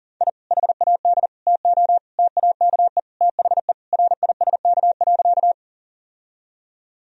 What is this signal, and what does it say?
Morse code, heard as a keyed tone, about 40 wpm, 700 Hz. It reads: I HAD TO TAKE THE RISK.
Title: Sentences from Top 1000 Words - Review Part 4 - 40wpm